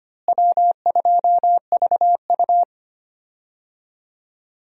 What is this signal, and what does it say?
Morse code, heard as a keyed tone, about 25 wpm, 700 Hz. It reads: W2VU
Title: USA Call Signs - 4 Characters - Repeated 2x - 25wpm